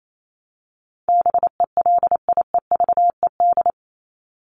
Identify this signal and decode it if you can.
Morse code, heard as a keyed tone, about 28 wpm, 700 Hz. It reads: BELIEVED
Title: ED Suffix - Top 100 Words with Base Words 28wpm